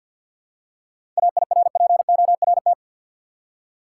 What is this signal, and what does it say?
Morse code, heard as a keyed tone, about 52 wpm, 700 Hz. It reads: AIRPORT
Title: Seven Letter Words - 50wpm